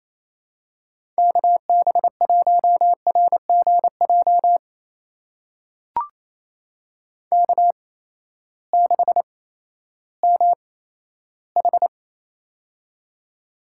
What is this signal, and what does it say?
Morse code, heard as a keyed tone, about 28 wpm, spaced as for 14 wpm, 700 Hz. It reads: KB1RGJ K6MH
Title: USA Call Signs - 8x Character Spacing - 28wpm